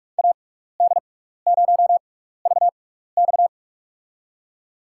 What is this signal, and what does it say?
Morse code, heard as a keyed tone, about 46 wpm, spaced as for 15 wpm, 700 Hz. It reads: AD0VX
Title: USA Call Signs - 5 Characters - 6x Character Spacing - 45wpm